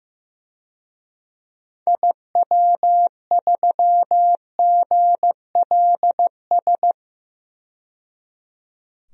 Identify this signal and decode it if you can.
Morse code, heard as a keyed tone, about 15 wpm, 700 Hz. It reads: IW3GLS
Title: DXCC-100 Call Signs - 6 Characters - 15wpm